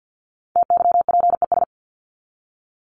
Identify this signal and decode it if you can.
Morse code, heard as a keyed tone, about 50 wpm, 700 Hz. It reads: TYPES
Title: ES:S Suffix - Top 100 Words with Base Words 50wpm